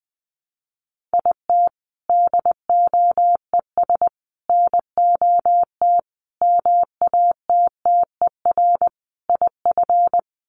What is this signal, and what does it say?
Morse code, heard as a keyed tone, about 20 wpm, 700 Hz. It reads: IT DOES NOT MATTER IF
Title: Sets of 5 Words Encore 20wpm